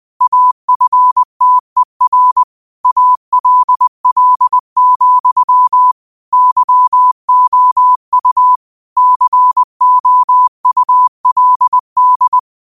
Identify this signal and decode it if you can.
Morse code, heard as a keyed tone, about 20 wpm, 1000 Hz. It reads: AFTER ALL, YOU COULD